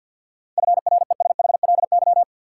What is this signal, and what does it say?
Morse code, heard as a keyed tone, about 52 wpm, 700 Hz. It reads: FRESHLY
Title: Sets of 5 Words Encore - New Word Drill - 50wpm